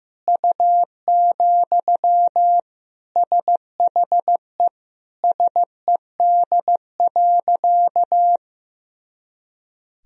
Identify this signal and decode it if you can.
Morse code, heard as a keyed tone, about 15 wpm, 700 Hz. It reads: U, SHE SED.
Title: Sentences from Top 400 Words and Abbreviations 15wpm